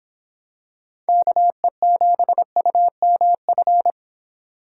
Morse code, heard as a keyed tone, about 26 wpm, 700 Hz. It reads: KE7UMF